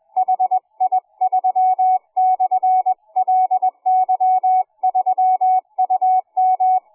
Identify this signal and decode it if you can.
Morse code, heard as a keyed tone, about 20 wpm, 740 Hz. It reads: HI3/LY3UM